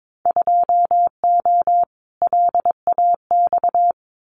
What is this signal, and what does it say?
Morse code, heard as a keyed tone, about 22 wpm, 700 Hz. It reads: 2O LAX
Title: Winter Field Day Exchanges 22wpm